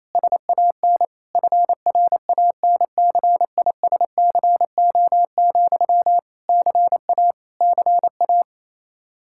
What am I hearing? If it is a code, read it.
Morse code, heard as a keyed tone, about 28 wpm, 700 Hz. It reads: SAN FRANCISCO, CA CA
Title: Practice QSO - Exchange 1 - Beginning - 28wpm